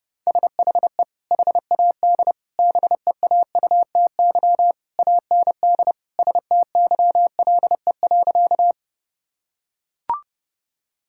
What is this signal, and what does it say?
Morse code, heard as a keyed tone, about 32 wpm, 700 Hz. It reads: SHE HAD BEAUTY AND STYLE.
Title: Sentences from Top 1000 Words - Review Part 5 - 30wpm